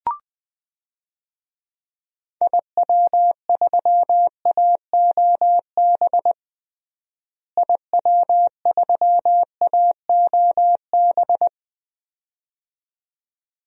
Morse code, heard as a keyed tone, about 20 wpm, 700 Hz. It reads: IW3AOB IW3AOB
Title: DXCC-100 Call Signs - 6 Characters - Repeated 2x - 20wpm